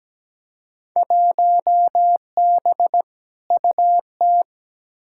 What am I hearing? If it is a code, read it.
Morse code, heard as a keyed tone, about 17 wpm, 700 Hz. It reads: 1B UT
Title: Field Day Exchanges - 17wpm